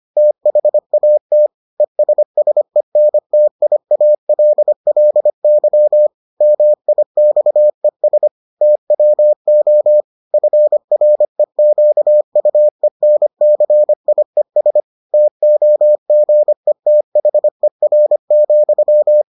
Morse code, heard as a keyed tone, about 25 wpm, 600 Hz. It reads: THAT ESSENTIALLY MIXES TWO FREQUENCIES TOGETHER,